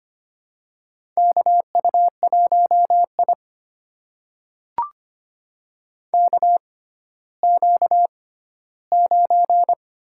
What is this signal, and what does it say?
Morse code, heard as a keyed tone, about 25 wpm, spaced as for 14 wpm, 700 Hz. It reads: KU1I KQ9
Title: USA Call Signs - 4 Characters - 6x Character Spacing - 25wpm